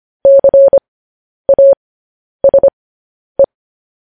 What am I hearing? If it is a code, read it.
Morse code, heard as a keyed tone, about 25 wpm, spaced as for 10 wpm, 550 Hz. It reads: CASE
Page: news headlines in morse code at 10 wpm